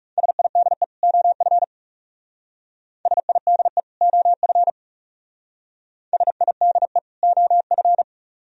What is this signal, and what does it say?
Morse code, heard as a keyed tone, about 36 wpm, 700 Hz. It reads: SIDE OF SIDE OF SIDE OF
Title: Speed-Racing - Sets of 2 Words 30wpm